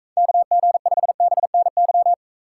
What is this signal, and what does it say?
Morse code, heard as a keyed tone, about 42 wpm, 700 Hz. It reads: KG5BNY